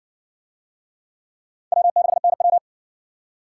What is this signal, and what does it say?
Morse code, heard as a keyed tone, about 53 wpm, 700 Hz. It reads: W6NW